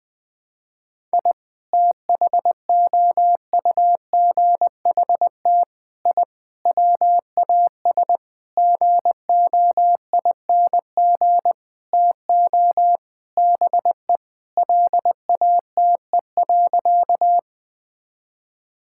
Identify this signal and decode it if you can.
Morse code, heard as a keyed tone, about 20 wpm, 700 Hz. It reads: I THOUGHT I WAS GOING TO BE LATE.
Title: Sentences with 6-Word Phrases - 20wpm